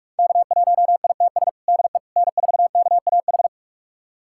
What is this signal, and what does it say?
Morse code, heard as a keyed tone, about 46 wpm, 700 Hz. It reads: K1ITS DE N4KAH
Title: USA Callsign DE Callsign - 5 Characters - 45wpm